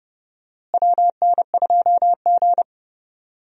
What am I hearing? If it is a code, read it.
Morse code, heard as a keyed tone, about 30 wpm, 700 Hz. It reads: WN2G
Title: USA Call Signs - 4 Characters - Repeated 2x - 30wpm